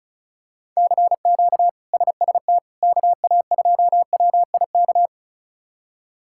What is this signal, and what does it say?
Morse code, heard as a keyed tone, about 36 wpm, 700 Hz. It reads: CQ SST KA2WIK
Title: K1USN SST QSO Practice - 35wpm